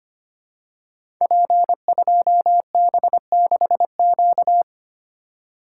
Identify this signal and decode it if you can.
Morse code, heard as a keyed tone, about 25 wpm, 700 Hz. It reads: P2B6Q
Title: Code Groups - 5 Letter:Numbers - 25wpm